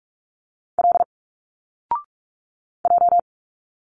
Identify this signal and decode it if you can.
Morse code, heard as a keyed tone, about 46 wpm, 700 Hz. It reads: L J
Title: Single Letters 45wpm